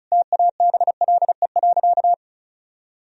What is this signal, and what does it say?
Morse code, heard as a keyed tone, about 36 wpm, 700 Hz. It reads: TABLE.
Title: Sentences from Top 1000 Words - Review Part 2 - 35wpm